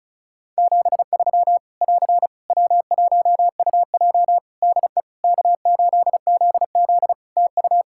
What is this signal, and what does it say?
Morse code, heard as a keyed tone, about 36 wpm, 700 Hz. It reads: 73 <AR> W1UJ DE K8ZZ TU
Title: Practice QSO - Exchange 4 - Ending - 35wpm